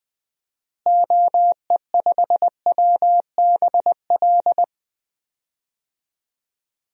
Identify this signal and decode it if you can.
Morse code, heard as a keyed tone, about 21 wpm, 700 Hz. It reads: OE5WBL
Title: DXCC-100 Call Signs - 6 Characters - Repeated 2x - 20wpm